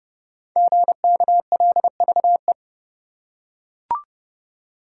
Morse code, heard as a keyed tone, about 28 wpm, 700 Hz. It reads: GKLVE
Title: Code Groups - 5 Letter:Numbers - 30wpm